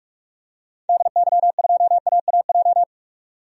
Morse code, heard as a keyed tone, about 46 wpm, 700 Hz. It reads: DY2AAJ
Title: DXCC-100 Call Signs - 6 Characters - 45wpm